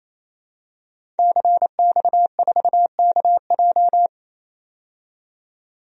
Morse code, heard as a keyed tone, about 28 wpm, 700 Hz. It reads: CX4KJ
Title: DXCC-100 Call Signs - 5 Characters - 28wpm